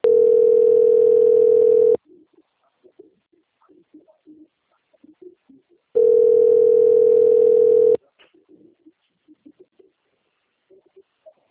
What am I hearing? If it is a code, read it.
Ringback tone (North America), 440 + 480 Hz.